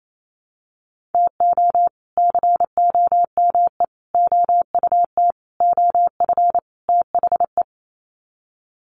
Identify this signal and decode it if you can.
Morse code, heard as a keyed tone, about 28 wpm, 700 Hz. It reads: TO COME OUT OF THE
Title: Sets of 5 Words Encore2 28wpm